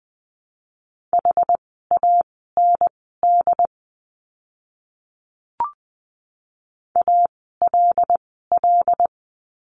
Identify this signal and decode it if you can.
Morse code, heard as a keyed tone, about 20 wpm, spaced as for 14 wpm, 700 Hz. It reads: HAND ALL
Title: 115 Lesson 19f - Letter w - Words 20wpm